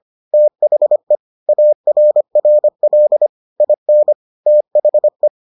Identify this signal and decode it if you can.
Morse code, heard as a keyed tone, about 25 wpm, 600 Hz. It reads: THE ARRL IN THE